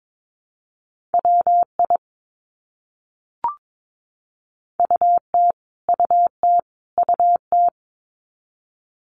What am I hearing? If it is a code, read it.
Morse code, heard as a keyed tone, about 22 wpm, 700 Hz. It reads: WI UT UT UT
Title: US States Repeated 3x 22wpm